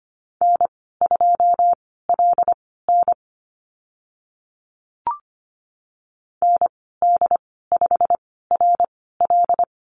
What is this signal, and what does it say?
Morse code, heard as a keyed tone, about 25 wpm, spaced as for 16 wpm, 700 Hz. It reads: N2LN ND5RL
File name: Lesson 15g - Number 5 - Call Signs 25wpm